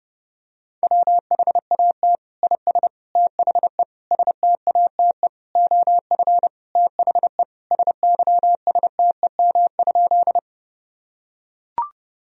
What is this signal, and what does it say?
Morse code, heard as a keyed tone, about 30 wpm, 700 Hz. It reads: WHAT IS THE STATE OF THE SYSTEM?
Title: Sentences from Top 1000 Words - Review Part 3 - 30wpm